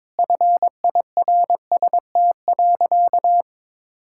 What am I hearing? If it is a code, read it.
Morse code, heard as a keyed tone, about 22 wpm, 700 Hz. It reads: FIRST.
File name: Sentences with 7-Word Phrases - 22wpm